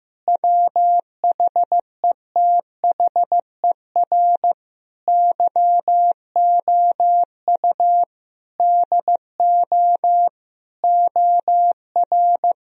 Morse code, heard as a keyed tone, about 15 wpm, 700 Hz. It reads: WHETHER YOU DO OR
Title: Sentences from Top 1000 Words - Review Part 3 - 15wpm